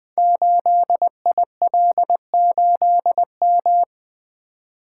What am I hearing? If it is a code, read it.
Morse code, heard as a keyed tone, about 20 wpm, 700 Hz. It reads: 8IL8M
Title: Code Groups - 5 Letter:Numbers - 20wpm